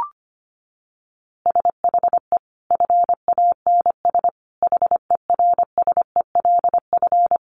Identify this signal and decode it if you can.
Morse code, heard as a keyed tone, about 25 wpm, 700 Hz. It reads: SHE FANS HERSELF
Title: Easy Phrases 25wpm